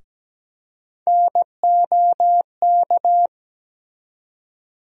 Morse code, heard as a keyed tone, about 17 wpm, 700 Hz. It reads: NOK
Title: USA Call Sign Postfixes - 3 Letters 17wpm